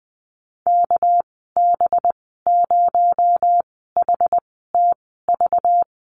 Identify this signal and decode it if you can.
Morse code, heard as a keyed tone, about 20 wpm, spaced as for 14 wpm, 700 Hz. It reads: KB0HTV
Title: 238 Lesson 38g - Letter q - Call Signs 20wpm